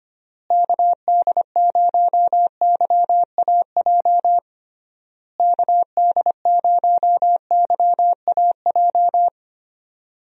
Morse code, heard as a keyed tone, about 25 wpm, 700 Hz. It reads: KD0YAJ KD0YAJ
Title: USA Call Signs - 6 Characters - Repeated 3x - 25wpm